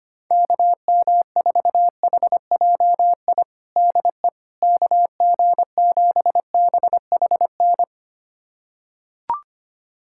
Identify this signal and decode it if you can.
Morse code, heard as a keyed tone, about 25 wpm, 700 Hz. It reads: KM4HJI DE KG7BHN
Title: USA Callsign DE Callsign - 6 Characters - 25wpm